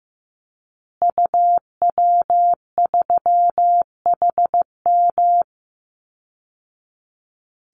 Morse code, heard as a keyed tone, about 15 wpm, 700 Hz. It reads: UW3HM
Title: Call Signs Repeated 3x - 15wpm